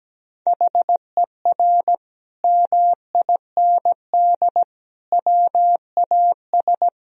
Morse code, heard as a keyed tone, about 17 wpm, 700 Hz. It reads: HER MIND WAS
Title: Sentences from Top 1000 Words - Review Part 2 - 17wpm